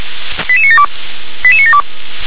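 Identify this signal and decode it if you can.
Five-tone selective call (ZVEI), heard as quick runs of five tones, tones of about 70 ms. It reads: 88972 88972
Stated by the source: ZVEI five-tone selective call 88786 received by SDR.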